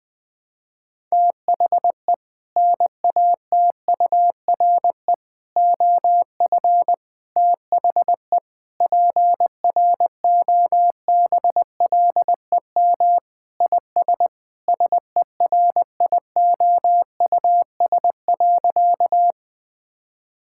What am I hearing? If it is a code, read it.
Morse code, heard as a keyed tone, about 20 wpm, 700 Hz. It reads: THE NATURE OF THE PROBLEM IS SERIOUS.
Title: Sentences with 3-Word Phrases - 20wpm